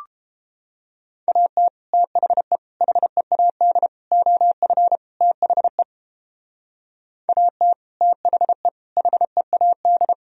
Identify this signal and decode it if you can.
Morse code, heard as a keyed tone, about 32 wpm, 700 Hz. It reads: AT THE HEAD OF THE AT THE HEAD